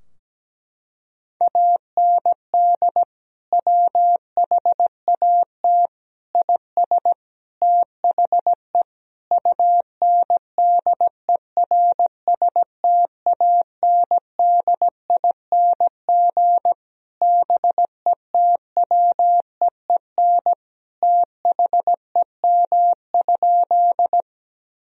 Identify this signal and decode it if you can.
Morse code, heard as a keyed tone, about 17 wpm, 700 Hz. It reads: AND WHAT IS THE UNDERSTANDING BETWEEN THEM?
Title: Sentences from Top 1000 Words - Review Part 3 - 17wpm